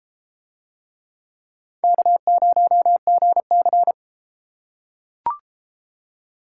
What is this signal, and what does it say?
Morse code, heard as a keyed tone, about 32 wpm, 700 Hz. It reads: K0GC